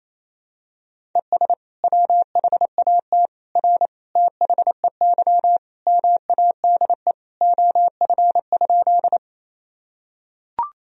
Morse code, heard as a keyed tone, about 28 wpm, 700 Hz. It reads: ES WHAT R THEY MADE OF?
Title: Sentences from Top 200 Words and Abbreviations 28wpm